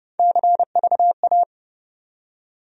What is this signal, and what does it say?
Morse code, heard as a keyed tone, about 30 wpm, 700 Hz. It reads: CVA
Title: USA Call Sign Postfixes - 3 Letters 30wpm